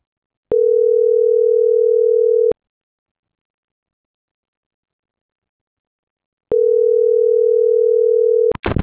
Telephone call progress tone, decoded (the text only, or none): ringback tone